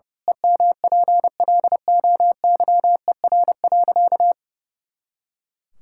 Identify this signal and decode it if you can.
Morse code, heard as a keyed tone, about 30 wpm, 700 Hz. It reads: EMPLOYER.
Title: Pangram Challenge v2 - 30wpm